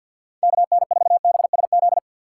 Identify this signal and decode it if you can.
Morse code, heard as a keyed tone, about 51 wpm, 700 Hz. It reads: KN4BSZ
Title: USA Call Signs - 6 Characters - 3x Character Spacing - 50wpm